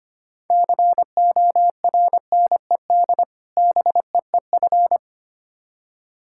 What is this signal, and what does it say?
Morse code, heard as a keyed tone, about 25 wpm, 700 Hz. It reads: CORNED BEEF
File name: practice / 25wpm / Difficult Words with Hints - 25wpm